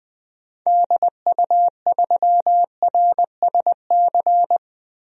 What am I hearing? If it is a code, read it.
Morse code, heard as a keyed tone, about 20 wpm, 700 Hz. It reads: DU3RSC
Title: DXCC-100 Call Signs - 6 Characters - 20wpm